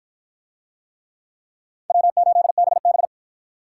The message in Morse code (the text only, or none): W8BB